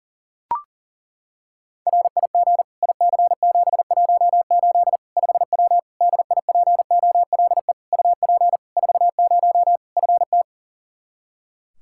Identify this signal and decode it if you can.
Morse code, heard as a keyed tone, about 40 wpm, 700 Hz. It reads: RIG IC718 5W DIPOLE UP 40 FT
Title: K1USN SST QSO Practice 40wpm